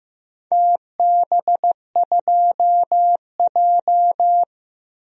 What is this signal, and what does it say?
Morse code, heard as a keyed tone, about 15 wpm, 700 Hz. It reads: TB2J
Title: DXCC-100 Call Signs - 4 Characters - 15wpm